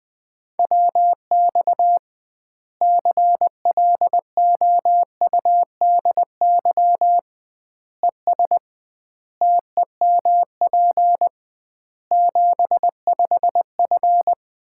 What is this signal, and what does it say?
Morse code, heard as a keyed tone, about 20 wpm, 700 Hz. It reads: WX CLOUDY ES TEMP 75F
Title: Common QSO Phrases - 2x Word-Spacing - 20wpm